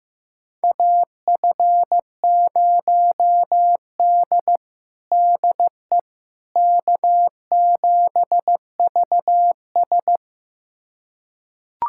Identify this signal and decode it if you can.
Morse code, heard as a keyed tone, about 15 wpm, 700 Hz. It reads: AF0D DE K7VS
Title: USA Callsign DE Callsign - 4 Characters - 15wpm